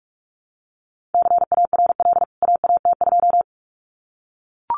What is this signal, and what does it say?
Morse code, heard as a keyed tone, about 44 wpm, 700 Hz. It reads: CARL AAT2
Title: CWT Exchange 45wpm